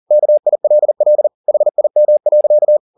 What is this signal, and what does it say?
Morse code, heard as a keyed tone, about 40 wpm, 600 Hz. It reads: KILL HIM.
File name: the old man and the sea 2 40wpm